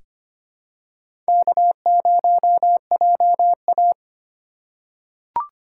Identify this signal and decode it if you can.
Morse code, heard as a keyed tone, about 25 wpm, 700 Hz. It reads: K0JA
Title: USA Call Signs - 4 Characters - 2x Character Spacing - 25wpm